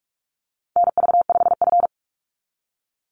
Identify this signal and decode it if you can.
Morse code, heard as a keyed tone, about 44 wpm, 700 Hz. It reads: NV5F
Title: Call Signs - 45wpm